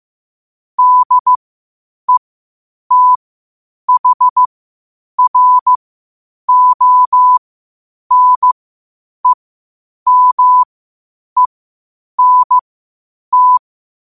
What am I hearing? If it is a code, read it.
Morse code, heard as a keyed tone, about 15 wpm, spaced as for 8 wpm, 1000 Hz. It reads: DETHRONEMENT
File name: qotdfiles / 8wpm